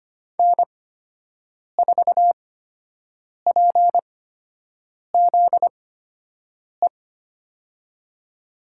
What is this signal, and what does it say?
Morse code, heard as a keyed tone, about 25 wpm, spaced as for 7 wpm, 700 Hz. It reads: N4PZE
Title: USA Call Signs - 5 Characters - 8x Character Spacing - 25wpm